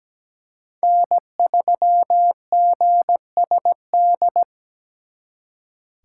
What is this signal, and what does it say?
Morse code, heard as a keyed tone, about 20 wpm, 700 Hz. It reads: N3GSD
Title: Speed Racing - USA 1x3 Call Signs 17wpm